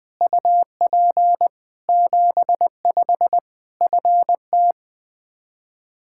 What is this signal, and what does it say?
Morse code, heard as a keyed tone, about 21 wpm, 700 Hz. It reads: UP 75 FT
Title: K1USN SST QSO Practice 20wpm